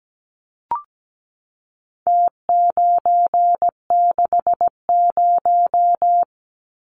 Morse code, heard as a keyed tone, about 17 wpm, 700 Hz. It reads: T960
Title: Numbers - Contest Style - 4-Digit 17wpm